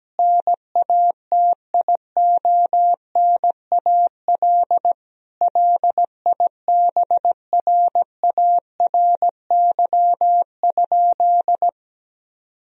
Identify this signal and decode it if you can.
Morse code, heard as a keyed tone, about 17 wpm, 700 Hz. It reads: NATIONAL LIBRARY?
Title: Sentences from Top 1000 Words - Review Part 4 - 17wpm